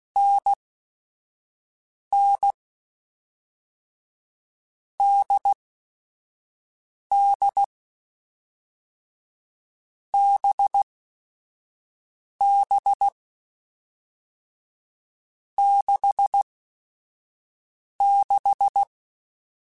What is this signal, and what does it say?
Morse code, heard as a keyed tone, about 16 wpm, spaced as for 4 wpm, 800 Hz. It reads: NNDDBB66
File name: Short Runs / Short Runs-mp3s / NDB NDB6 @5